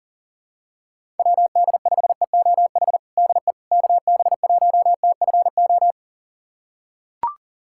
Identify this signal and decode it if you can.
Morse code, heard as a keyed tone, about 40 wpm, 700 Hz. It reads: WD5EOH DE KB1TFO
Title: USA Callsign DE Callsign - 6 Characters - 40wpm